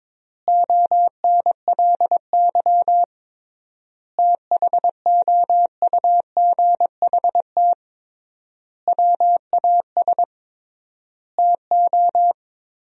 Morse code, heard as a keyed tone, about 22 wpm, 700 Hz. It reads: ONLY THOUGHT WAS TO
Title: 3x Word-Spacing - Sentences from Top 400 Words - 22wpm